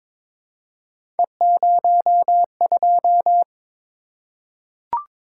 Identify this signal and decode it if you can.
Morse code, heard as a keyed tone, about 21 wpm, 700 Hz. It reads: E02
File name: practice / 22wpm / Three Letter-Number Combinations - 22wpm